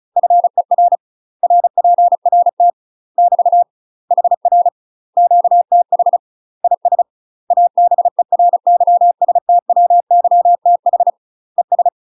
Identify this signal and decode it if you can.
Morse code, heard as a keyed tone, about 36 wpm, 700 Hz. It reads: FER RPRT <BT> HR QTH IS ABERYSTWYTH ES
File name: UK MASTER QSOs TRAINING / 35WPM / UK MASTER QSOs 0009